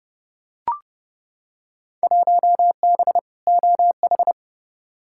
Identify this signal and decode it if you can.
Morse code, heard as a keyed tone, about 30 wpm, 700 Hz. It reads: 1B OH
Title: Field Day Exchanges - 30wpm